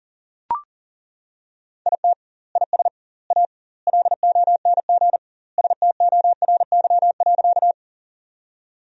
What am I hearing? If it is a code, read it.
Morse code, heard as a keyed tone, about 40 wpm, 700 Hz. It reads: IT IS A LONG STORY.
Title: 2x Word-Spacing - Sentences from Top 300 Words - 40wpm